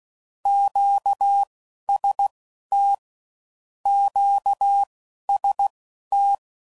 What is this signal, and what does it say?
Morse code, heard as a keyed tone, about 16 wpm, spaced as for 11 wpm, 800 Hz. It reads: QST QST
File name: QST de WA7RSO